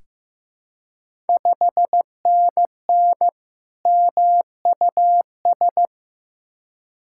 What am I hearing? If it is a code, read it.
Morse code, heard as a keyed tone, about 15 wpm, 700 Hz. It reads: 5NN MUS
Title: Ontario QSO Party Exchange 15wpm